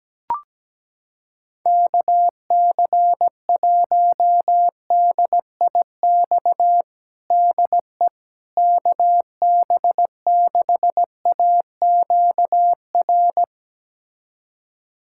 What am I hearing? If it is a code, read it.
Morse code, heard as a keyed tone, about 17 wpm, 700 Hz. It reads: KC1DIX DE KB6AQR
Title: USA Callsign DE Callsign 17wpm